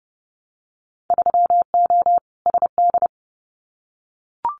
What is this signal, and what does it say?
Morse code, heard as a keyed tone, about 30 wpm, 700 Hz. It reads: 3O SD
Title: Winter Field Day Exchanges 30wpm